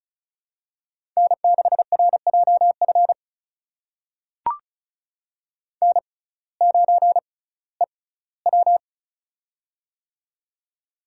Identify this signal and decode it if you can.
Morse code, heard as a keyed tone, about 36 wpm, spaced as for 19 wpm, 700 Hz. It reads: N6RJF N9EW
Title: USA Call Signs - 6x Character Spacing - 35wpm